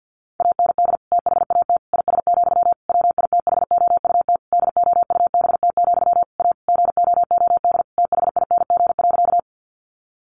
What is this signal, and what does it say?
Morse code, heard as a keyed tone, about 50 wpm, 700 Hz. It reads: AND THAT IS, WITHOUT DOUBT, A GOOD THING.
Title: Sentences from Top 1500 Words 50wpm